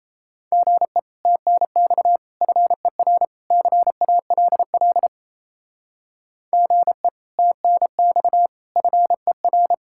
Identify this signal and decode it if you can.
Morse code, heard as a keyed tone, about 33 wpm, 700 Hz. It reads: GE TNX FER CALL GE TNX FER
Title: Speed-Racing Common QSO Phrases - 28wpm